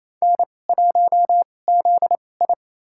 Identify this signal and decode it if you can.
Morse code, heard as a keyed tone, about 28 wpm, spaced as for 20 wpm, 700 Hz. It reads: N1ZI
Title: USA Call Signs - 4 Characters - 2x Character Spacing - 28wpm